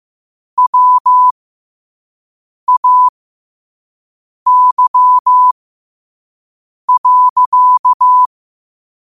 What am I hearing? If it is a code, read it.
Morse code, heard as a keyed tone, about 15 wpm, spaced as for 5 wpm, 1000 Hz. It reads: WAY.
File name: qotdfiles / 5wpm